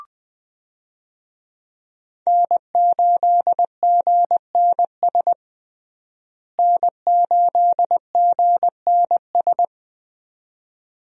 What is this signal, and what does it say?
Morse code, heard as a keyed tone, about 20 wpm, 700 Hz. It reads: N8GNS N8GNS
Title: USA Call Signs - 5 Characters - Repeated 2x - 20wpm